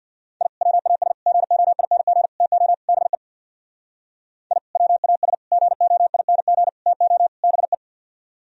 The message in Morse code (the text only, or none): I WAS GOING TO BE I WAS GOING TO BE